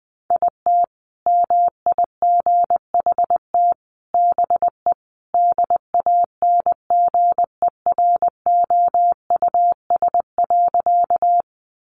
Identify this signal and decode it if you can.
Morse code, heard as a keyed tone, about 20 wpm, 700 Hz. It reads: IT MIGHT BE DANGEROUS.
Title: Sentences from Top 1200 Words 20wpm